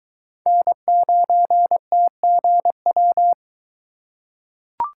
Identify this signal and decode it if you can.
Morse code, heard as a keyed tone, about 22 wpm, 700 Hz. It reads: N9TGW